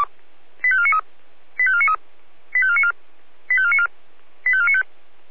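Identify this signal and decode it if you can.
Five-tone selective call (ZVEI), heard as quick runs of five tones, tones of about 70 ms. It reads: 85493 85493 85494 85494 85495
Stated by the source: ZVEI five-tone call received with an SDR.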